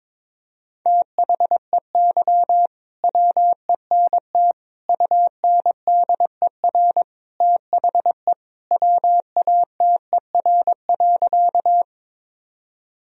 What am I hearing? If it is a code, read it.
Morse code, heard as a keyed tone, about 22 wpm, 700 Hz. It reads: THEY WENT UNDER THE WATER.
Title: Sentences from Top 1000 Words - Review Part 1 - 22wpm